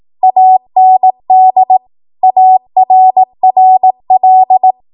Morse code, heard as a keyed tone, about 18 wpm, 750 Hz. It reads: AND ARRL
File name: Archive / 18 WPM / 18WPM